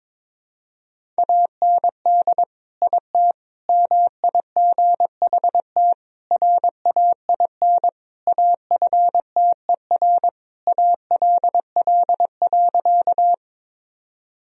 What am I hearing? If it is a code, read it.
Morse code, heard as a keyed tone, about 22 wpm, 700 Hz. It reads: AND IT MIGHT RAIN AFTER ALL.
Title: Sentences from Top 1000 Words - Review Part 1 - 22wpm